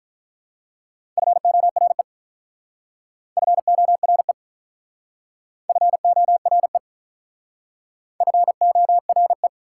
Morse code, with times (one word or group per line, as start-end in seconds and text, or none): 1.2-2.0 FORE
3.4-4.3 FORE
5.7-6.8 FORE
8.2-9.5 FORE